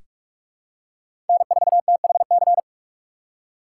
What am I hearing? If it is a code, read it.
Morse code, heard as a keyed tone, about 46 wpm, 700 Hz. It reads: N4THC
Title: USA Call Signs - 6x Character Spacing - 45wpm